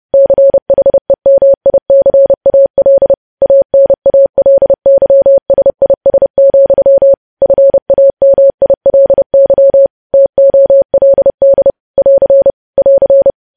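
Morse code, heard as a keyed tone, about 30 wpm, 550 Hz. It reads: CHEMICAL ANALYSIS, FAMILY TOLD <AR> <AR>
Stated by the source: Morse code news headlines sent at 30 words per minute.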